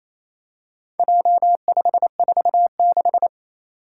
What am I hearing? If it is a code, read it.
Morse code, heard as a keyed tone, about 28 wpm, 700 Hz. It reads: J546
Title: Four Challenging Letter-Number Combinations 28wpm